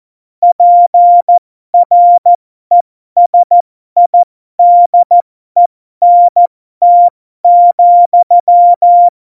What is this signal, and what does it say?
Morse code, heard as a keyed tone, about 14 wpm, spaced as for 12 wpm, 700 Hz. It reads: PRESIDENT,
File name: Morsekurs / Witze